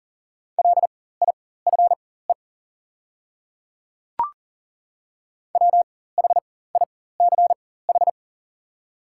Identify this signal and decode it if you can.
Morse code, heard as a keyed tone, about 40 wpm, spaced as for 19 wpm, 700 Hz. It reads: LIFE WHICH
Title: 4x Character Spacing - Top 300 Words - 40wpm